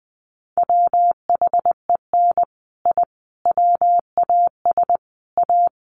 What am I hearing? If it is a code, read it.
Morse code, heard as a keyed tone, about 20 wpm, 700 Hz. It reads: WHEN I WAS A
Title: Sets of 5 Words Encore 20wpm